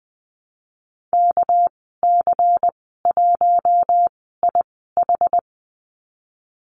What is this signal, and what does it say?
Morse code, heard as a keyed tone, about 20 wpm, spaced as for 15 wpm, 700 Hz. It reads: KC1IH
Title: 213 Lesson 34g - Number 8 - Call Signs 20wpm